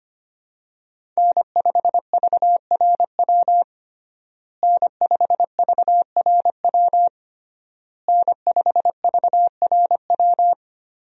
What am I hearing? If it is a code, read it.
Morse code, heard as a keyed tone, about 25 wpm, 700 Hz. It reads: N5VRW N5VRW N5VRW
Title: USA Call Signs - 5 Characters - Repeated 3x - 25wpm